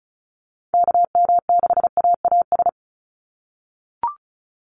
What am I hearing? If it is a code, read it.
Morse code, heard as a keyed tone, about 34 wpm, 700 Hz. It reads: KM6AAS
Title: USA Call Signs 35wpm